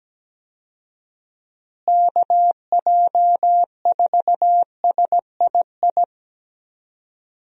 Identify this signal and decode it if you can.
Morse code, heard as a keyed tone, about 17 wpm, 700 Hz. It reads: KJ4SII
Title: USA Call Signs - 6 Characters - Repeated 2x - 17wpm